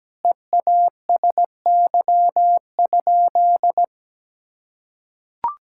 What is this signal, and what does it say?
Morse code, heard as a keyed tone, about 17 wpm, 700 Hz. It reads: EASY?
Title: Sentences from Top 1000 Words - Review Part 5 - 17wpm